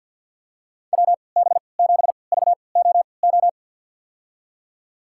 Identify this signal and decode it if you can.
Morse code, heard as a keyed tone, about 53 wpm, spaced as for 29 wpm, 700 Hz. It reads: WB7VOO
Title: USA Call Signs - 6 Characters - 3x Character Spacing - 50wpm